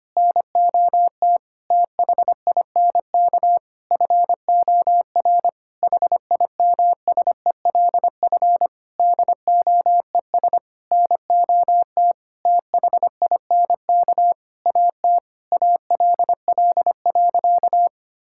Morse code, heard as a keyed tone, about 25 wpm, 700 Hz. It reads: NOT THINK FOR HIMSELF DOES NOT THINK AT ALL.
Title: Sentences from Top 1600 Words 25wpm